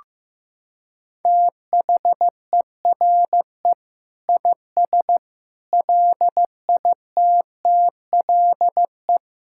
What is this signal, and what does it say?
Morse code, heard as a keyed tone, about 15 wpm, 700 Hz. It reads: THERE IS LITTLE
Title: Sentences from Top 1000 Words - Review Part 6 - 15wpm